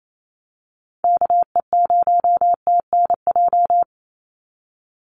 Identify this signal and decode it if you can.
Morse code, heard as a keyed tone, about 28 wpm, 700 Hz. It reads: KE0TNJ
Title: USA Call Signs 28wpm